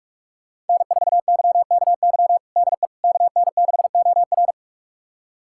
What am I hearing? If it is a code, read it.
Morse code, heard as a keyed tone, about 42 wpm, 700 Hz. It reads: N4YKY DE KN6OR
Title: USA Callsign DE Callsign 45wpm